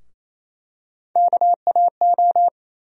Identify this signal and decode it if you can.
Morse code, heard as a keyed tone, about 28 wpm, 700 Hz. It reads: KAO
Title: USA Call Sign Postfixes - 3 Letters 28wpm